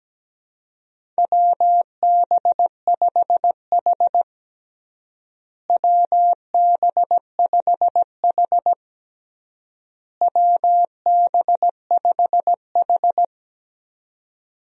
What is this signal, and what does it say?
Morse code, heard as a keyed tone, about 17 wpm, 700 Hz. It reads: WB5H WB5H WB5H
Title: USA Call Signs - 4 Characters - Repeated 3x - 17wpm